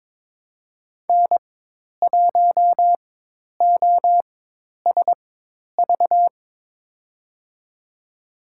Morse code, heard as a keyed tone, about 23 wpm, spaced as for 10 wpm, 700 Hz. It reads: N1OSV
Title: USA Call Signs - 5 Characters - 4x Character Spacing - 22wpm